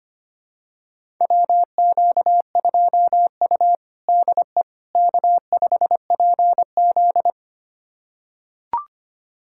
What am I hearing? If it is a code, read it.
Morse code, heard as a keyed tone, about 25 wpm, 700 Hz. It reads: WQ2U DE K5PZ
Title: USA Callsign DE Callsign - 4 Characters - 25wpm